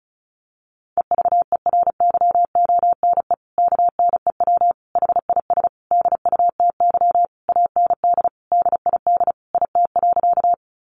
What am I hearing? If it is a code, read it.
Morse code, heard as a keyed tone, about 34 wpm, 700 Hz. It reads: EVERYONE KNEW HIS DUTY AND DID IT.
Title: Sentences from Top 1500 Words 35wpm